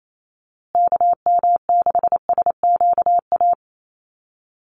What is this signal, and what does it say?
Morse code, heard as a keyed tone, about 28 wpm, 700 Hz. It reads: KM6SQA
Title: USA Call Signs Repeated 2x 28wpm